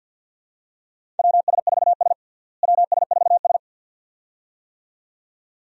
Morse code, heard as a keyed tone, about 47 wpm, 700 Hz. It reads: WS4S WS4S
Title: USA Call Signs - 4 Characters - Repeated 2x - 50wpm